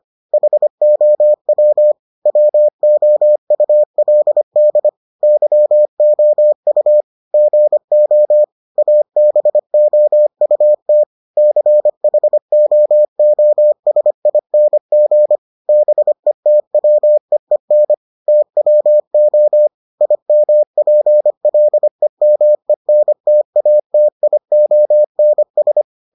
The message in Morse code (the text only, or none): HOW WOULD YOU GO ABOUT CHOOSING BETWEEN TWO IMPLEMENTATIONS